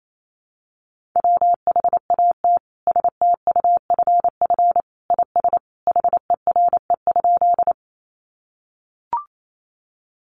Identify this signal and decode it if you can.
Morse code, heard as a keyed tone, about 28 wpm, 700 Hz. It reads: WHAT STUFF IS HERE?
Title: Sentences from Top 1400 Words 28wpm